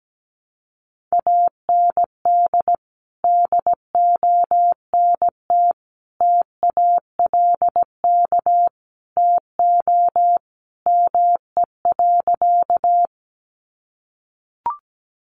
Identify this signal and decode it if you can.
Morse code, heard as a keyed tone, about 17 wpm, 700 Hz. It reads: AND DONT TALK TO ME.
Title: Sentences from Top 400 Words 17wpm